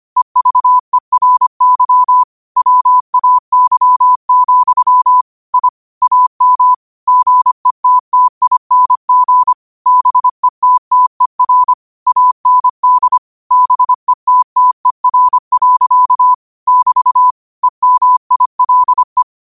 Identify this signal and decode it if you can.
Morse code, heard as a keyed tone, about 25 wpm, 1000 Hz. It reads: EVERY WAY, I AM GETTING BETTER AND BETTER. <BT> EMILE
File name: qotdfiles / 25wpm